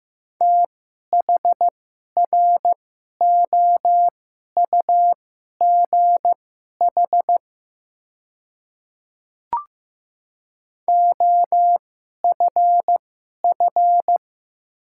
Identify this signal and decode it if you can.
Morse code, heard as a keyed tone, about 16 wpm, spaced as for 11 wpm, 700 Hz. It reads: THROUGH OFF
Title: 2x Character Spacing - Top 200 Words - 15wpm